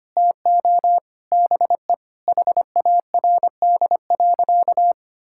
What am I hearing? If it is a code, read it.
Morse code, heard as a keyed tone, about 25 wpm, 700 Hz. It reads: TO BE HARD.
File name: practice / 25wpm / Sentences with 4-Word Phrases - 25wpm